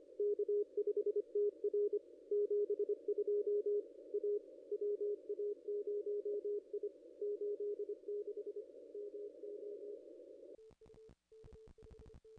Morse code, heard as a keyed tone, about 25 wpm, 440 Hz. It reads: K5TR 72 A WA0I 86 MO
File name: contests / ss-cw-k5tr-op-n5rz -0200